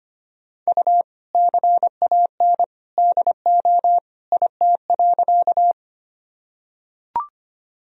Morse code, heard as a keyed tone, about 25 wpm, 700 Hz. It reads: U CAN DO IT.
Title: Sentences from Top 100 Words and Abbreviations 25wpm